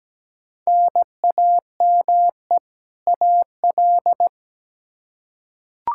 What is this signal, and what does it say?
Morse code, heard as a keyed tone, about 17 wpm, 700 Hz. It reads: NAME AL
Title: Practice QSO - Exchange 1 - Beginning - 17wpm